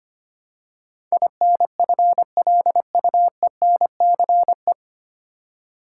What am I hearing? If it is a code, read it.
Morse code, heard as a keyed tone, about 25 wpm, 700 Hz. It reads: INFLUENCE
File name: practice / 25wpm / 9 Letter Words - First 3 Letters Given - 25wpm